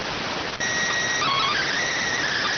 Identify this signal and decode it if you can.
SSTV (slow-scan television), PD 290.